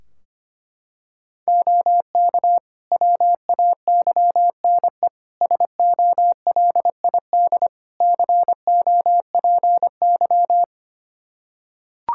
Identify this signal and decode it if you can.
Morse code, heard as a keyed tone, about 25 wpm, 700 Hz. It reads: OK WAYNE SOLID COPY
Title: Practice QSO - Exchange 3 - More Info - 25wpm